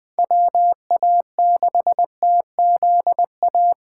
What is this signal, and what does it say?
Morse code, heard as a keyed tone, about 21 wpm, 700 Hz. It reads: WA6TZA
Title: Speed Racing - USA Call Signs 17wpm